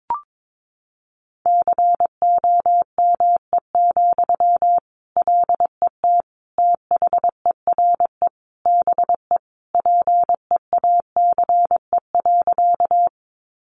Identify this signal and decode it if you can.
Morse code, heard as a keyed tone, about 22 wpm, 700 Hz. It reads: COME, LET THERE BE PEACE.
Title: Sentences from Top 1500 Words 22wpm